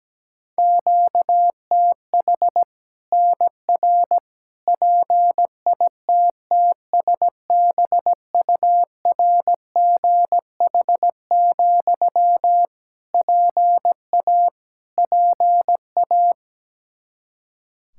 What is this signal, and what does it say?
Morse code, heard as a keyed tone, about 17 wpm, 700 Hz. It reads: QTH NR PITTSBURGH, PA PA
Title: Practice QSO - Exchange 1 - Beginning - 17wpm